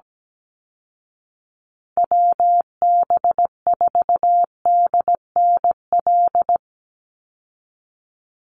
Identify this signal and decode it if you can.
Morse code, heard as a keyed tone, about 17 wpm, 700 Hz. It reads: WB4DNL
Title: Call Signs Repeated 2x - 17wpm